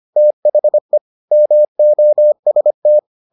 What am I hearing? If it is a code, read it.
Morse code, heard as a keyed tone, about 25 wpm, 600 Hz. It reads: THE MOST